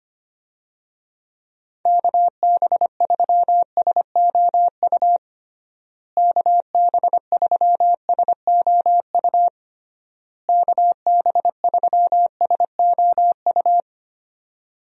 Morse code, heard as a keyed tone, about 25 wpm, 700 Hz. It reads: KB3SOU KB3SOU KB3SOU
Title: USA Call Signs - 6 Characters - Repeated 3x - 25wpm